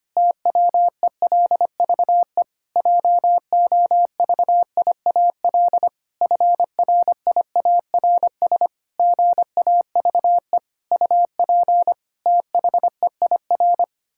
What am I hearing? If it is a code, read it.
Morse code, heard as a keyed tone, about 25 wpm, 700 Hz. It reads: TWELVE JOVIAL FRIARS GAVE UP THEIR
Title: Pangram Challenge v2 - 25wpm